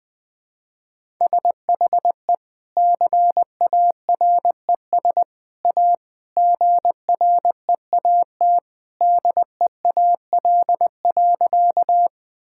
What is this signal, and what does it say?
Morse code, heard as a keyed tone, about 20 wpm, 700 Hz. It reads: SHE CARES A GREAT DEAL.
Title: Sentences with 3-Word Phrases - 20wpm